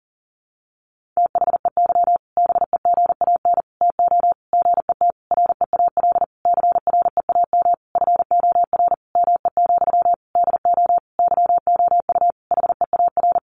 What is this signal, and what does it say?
Morse code, heard as a keyed tone, about 40 wpm, 700 Hz. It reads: THEY BEGAN TO GET REAL CREAM FOR ME, DO YOU HEAR
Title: Sentences from Top 1500 Words 40wpm